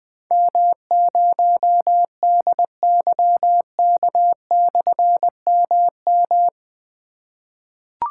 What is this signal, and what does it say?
Morse code, heard as a keyed tone, about 20 wpm, 700 Hz. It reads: M0DYK/MM
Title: Difficult Call Signs 20wpm